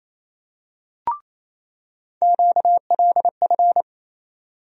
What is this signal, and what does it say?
Morse code, heard as a keyed tone, about 28 wpm, 700 Hz. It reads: QLF
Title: USA Call Sign Postfixes - 3 Letters 28wpm